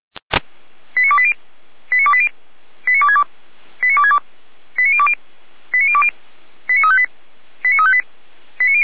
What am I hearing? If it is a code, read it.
Five-tone selective call (ZVEI), heard as quick runs of five tones, tones of about 70 ms. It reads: 89280 89280 89262 89262 89020 89020 89358 89358